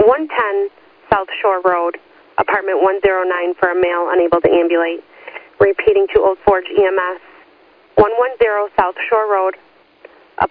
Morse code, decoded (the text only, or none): none